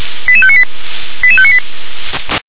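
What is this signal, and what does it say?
Five-tone selective call (ZVEI), heard as quick runs of five tones, tones of about 70 ms. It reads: 88598 88598
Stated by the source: ZVEI five-tone selective call received on SDR.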